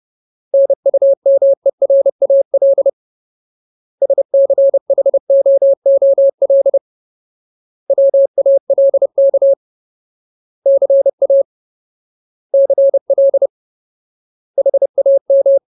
Morse code, heard as a keyed tone, about 30 wpm, 550 Hz. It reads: NUMERAL SCHOOL WALK CA CL HAM
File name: Ultimate Word List / 30WPM Ultimate